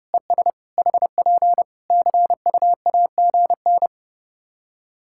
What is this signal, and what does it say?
Morse code, heard as a keyed tone, about 30 wpm, 700 Hz. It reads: ES HP CUAGN
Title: K1USN SST QSO Practice 30wpm